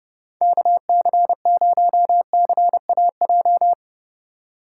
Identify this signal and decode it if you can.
Morse code, heard as a keyed tone, about 38 wpm, 700 Hz. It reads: KC0CAJ